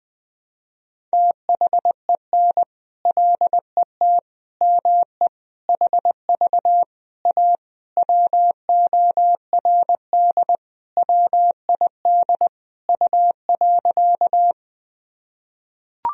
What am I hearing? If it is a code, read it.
Morse code, heard as a keyed tone, about 21 wpm, 700 Hz. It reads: THEN LET ME HV A WORD WID U.
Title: Sentences from Top 300 Words and Abbreviations 20wpm